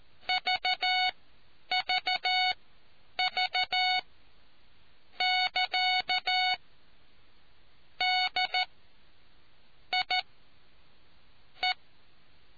Morse code, heard as a keyed tone, about 14 wpm, spaced as for 6 wpm, 740 Hz. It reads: VVV<KA>DIE